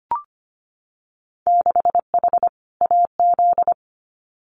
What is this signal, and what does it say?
Morse code, heard as a keyed tone, about 25 wpm, 700 Hz. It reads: 6H AZ
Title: Winter Field Day Exchanges Encore 25wpm